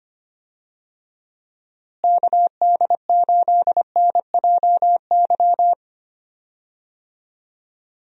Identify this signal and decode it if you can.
Morse code, heard as a keyed tone, about 25 wpm, 700 Hz. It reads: KD8NJY
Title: USA Call Signs - 6 Characters - Repeated 2x - 25wpm